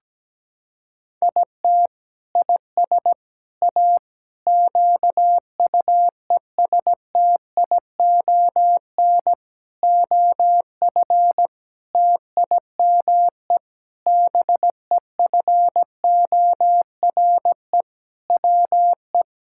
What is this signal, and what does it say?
Morse code, heard as a keyed tone, about 17 wpm, 700 Hz. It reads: IT IS A QUESTION OF TIME BEFORE WE
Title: Sentences from Top 1000 Words - Review Part 6 - 17wpm